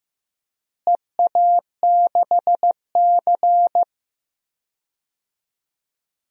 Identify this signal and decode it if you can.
Morse code, heard as a keyed tone, about 15 wpm, 700 Hz. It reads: EA6C
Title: DXCC-100 Call Signs - 4 Characters - Repeated 2x - 15wpm